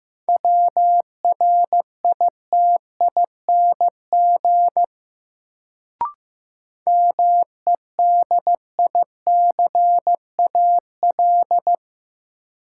Morse code, heard as a keyed tone, about 15 wpm, 700 Hz. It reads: WRITING MEDICAL
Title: Sentences from Top 800 Words - New Word Drill - 15wpm